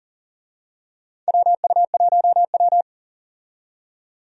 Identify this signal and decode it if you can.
Morse code, heard as a keyed tone, about 40 wpm, 700 Hz. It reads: WU1W
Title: USA Call Signs - 4 Characters - 40wpm